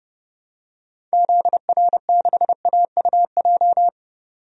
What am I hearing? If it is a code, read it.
Morse code, heard as a keyed tone, about 30 wpm, 700 Hz. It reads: ZR6AUJ
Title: DXCC-100 Call Signs - 6 Characters - 30wpm